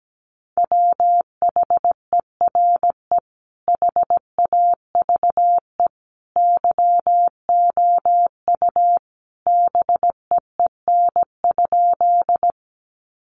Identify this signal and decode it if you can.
Morse code, heard as a keyed tone, about 17 wpm, 700 Hz. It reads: WHERE HAVE YOU BEEN?
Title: Sentences from Top 200 Words 17wpm